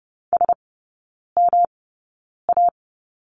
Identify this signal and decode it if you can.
Morse code, heard as a keyed tone, about 30 wpm, spaced as for 9 wpm, 700 Hz. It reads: SMA
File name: Lesson 19e - Letter w - Three Characters 30wpm